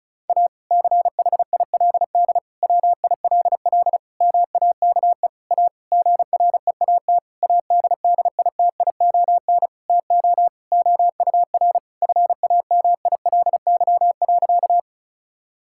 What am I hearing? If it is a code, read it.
Morse code, heard as a keyed tone, about 36 wpm, 700 Hz. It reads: A CHILD WILL MAKE A GREAT ADDITION TO OUR FAMILY.
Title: Sentences from Top 1000 Words - Review Part 6 - 35wpm